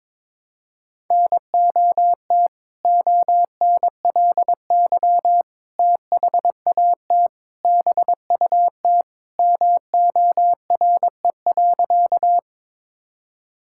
Morse code, heard as a keyed tone, about 22 wpm, 700 Hz. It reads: NOT ONLY THAT BUT MORE.
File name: practice / 22wpm / Sentences with 2-Word Phrases - 22wpm